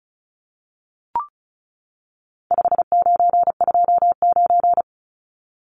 Morse code, heard as a keyed tone, about 34 wpm, 700 Hz. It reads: 5929
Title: Numbers - 4-Digit 35wpm